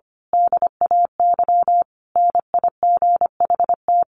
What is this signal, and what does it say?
Morse code, heard as a keyed tone, about 25 wpm, 700 Hz. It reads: DAY NIGHT
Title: A Sound Of Thunder 25wpm